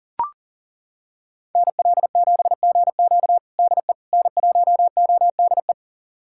Morse code, heard as a keyed tone, about 37 wpm, 700 Hz. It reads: NL7GQ DE N1ODE
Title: USA Callsign DE Callsign - 5 Characters - 40wpm